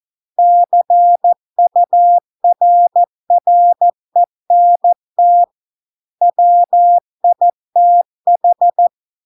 Morse code, heard as a keyed tone, about 14 wpm, 700 Hz. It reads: CURRENT WITH